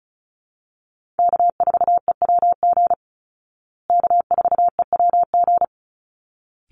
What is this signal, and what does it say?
Morse code, heard as a keyed tone, about 34 wpm, 700 Hz. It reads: K4EWG K4EWG
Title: Call Signs Repeated 3x - 35wpm